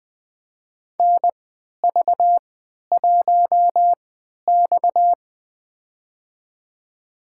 Morse code, heard as a keyed tone, about 20 wpm, spaced as for 11 wpm, 700 Hz. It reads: NV1X
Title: USA Call Signs - 4 Characters - 3x Character Spacing - 20wpm